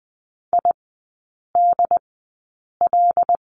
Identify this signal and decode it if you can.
Morse code, heard as a keyed tone, about 20 wpm, spaced as for 8 wpm, 690 Hz. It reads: IDL
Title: 145 Lesson 24e - Letter y - Three Characters 20wpm